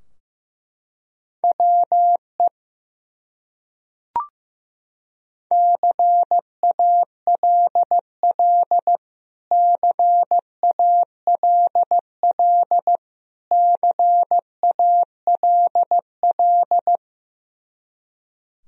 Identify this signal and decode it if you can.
Morse code, heard as a keyed tone, about 16 wpm, 700 Hz. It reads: WE CALL CALL CALL
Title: Top 100 Words Repeated 15wpm